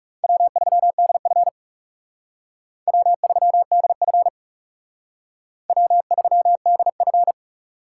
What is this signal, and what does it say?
Morse code, heard as a keyed tone, about 38 wpm, 700 Hz. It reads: W3DF W3DF W3DF